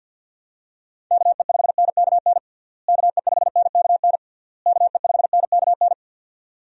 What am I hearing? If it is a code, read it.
Morse code, heard as a keyed tone, about 51 wpm, 700 Hz. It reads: KE5NKN KE5NKN KE5NKN
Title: USA Call Signs - 6 Characters - Repeated 3x - 50wpm